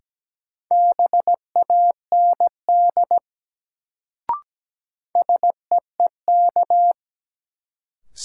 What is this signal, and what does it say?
Morse code, heard as a keyed tone, about 17 wpm, 700 Hz. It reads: BAND SEEK
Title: Sentences from Top 1600 Words - New Word Drill - 17wpm